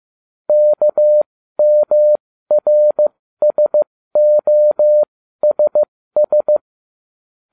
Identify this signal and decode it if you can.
Morse code, heard as a keyed tone, about 15 wpm, spaced as for 13 wpm, 600 Hz. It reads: KMRSOSS